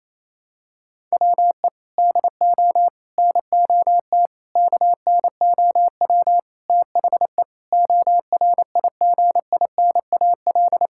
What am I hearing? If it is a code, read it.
Morse code, heard as a keyed tone, about 28 wpm, 700 Hz. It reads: WE DO NOT KNOW THE ORIGINAL
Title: Sentences from Top 1000 Words - Review Part 6 - 28wpm